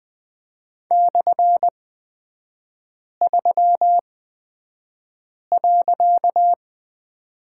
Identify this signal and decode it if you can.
Morse code, heard as a keyed tone, about 20 wpm, spaced as for 5 wpm, 700 Hz. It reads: /3.
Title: 255 Lesson 41e - Period - Three Characters 20wpm